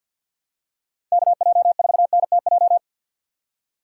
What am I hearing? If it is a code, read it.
Morse code, heard as a keyed tone, about 52 wpm, 700 Hz. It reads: KJ4NTJ
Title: USA Call Signs - 6 Characters - 3x Character Spacing - 50wpm